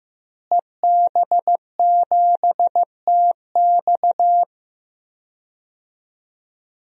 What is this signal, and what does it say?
Morse code, heard as a keyed tone, about 15 wpm, 700 Hz. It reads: EB7TX
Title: DXCC-100 Call Signs - 5 Characters - Repeated 2x - 15wpm